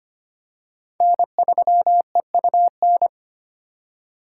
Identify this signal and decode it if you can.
Morse code, heard as a keyed tone, about 25 wpm, 700 Hz. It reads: N3EUN
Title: USA Call Signs - 5 Characters - 4x Character Spacing - 25wpm